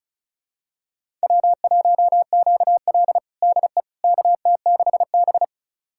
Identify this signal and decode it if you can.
Morse code, heard as a keyed tone, about 36 wpm, 700 Hz. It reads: W1QL DE KT6B
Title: USA Callsign DE Callsign - 4 Characters - 35wpm